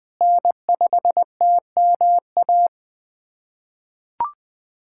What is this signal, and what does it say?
Morse code, heard as a keyed tone, about 20 wpm, 700 Hz. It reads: N5TMA
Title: USA Callsign DE Callsign - 5 Characters - 20wpm